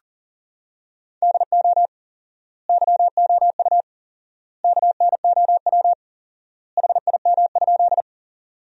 Morse code, heard as a keyed tone, about 40 wpm, 700 Hz. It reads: DO YOU KNOW HIM?
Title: 4x Word-Spacing - Sentences from Top 100 Words - 40wpm